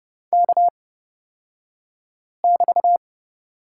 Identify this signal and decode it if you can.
Morse code, heard as a keyed tone, about 29 wpm, spaced as for 5 wpm, 700 Hz. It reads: K<BT>
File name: Lesson 36d - BT - Two Characters 30wpm